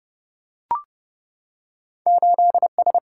Morse code, heard as a keyed tone, about 30 wpm, 700 Hz. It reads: 8S
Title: Single-Letter Single-Number 30wpm